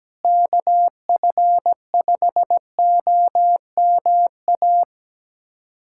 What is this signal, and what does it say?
Morse code, heard as a keyed tone, about 17 wpm, 700 Hz. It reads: KF5OMA
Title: USA Callsign DE Callsign 17wpm